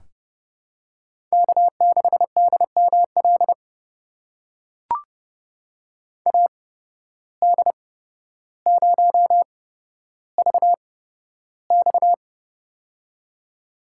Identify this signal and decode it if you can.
Morse code, heard as a keyed tone, about 31 wpm, spaced as for 13 wpm, 700 Hz. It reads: K6DML AD0VX
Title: USA Call Signs - 5 Characters - 8x Character Spacing - 30wpm